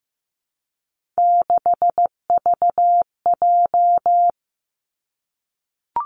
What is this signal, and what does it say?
Morse code, heard as a keyed tone, about 15 wpm, 700 Hz. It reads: 6VJ
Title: Three Challenging Letter-Number Combinations 15wpm